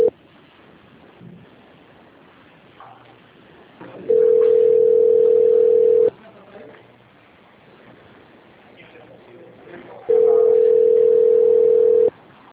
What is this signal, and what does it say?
Ringback tone (North America), 440 + 480 Hz.